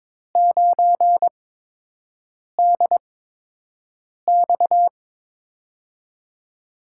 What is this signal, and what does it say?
Morse code, heard as a keyed tone, about 22 wpm, spaced as for 6 wpm, 700 Hz. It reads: 9DX
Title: USA Call Signs - 5 Characters - 8x Character Spacing - 22wpm